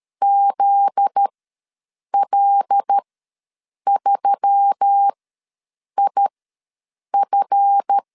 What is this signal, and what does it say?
Morse code, heard as a keyed tone, about 13 wpm, spaced as for 7 wpm, 800 Hz. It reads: ZL3IF
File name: 7 GF9Z ws 12wpm